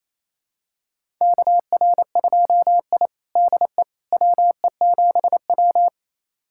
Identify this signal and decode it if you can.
Morse code, heard as a keyed tone, about 28 wpm, 700 Hz. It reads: KR2I DE WE7W
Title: USA Callsign DE Callsign - 4 Characters - 28wpm